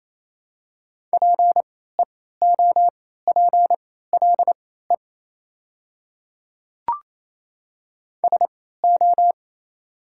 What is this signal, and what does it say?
Morse code, heard as a keyed tone, about 27 wpm, spaced as for 16 wpm, 690 Hz. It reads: PEOPLE SO
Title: 3x Character Spacing - Top 200 Words - 28wpm